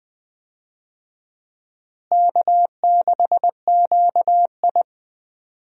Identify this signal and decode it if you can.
Morse code, heard as a keyed tone, about 20 wpm, 700 Hz. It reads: K6QI
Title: USA Call Signs - 4 Characters - Repeated 3x - 20wpm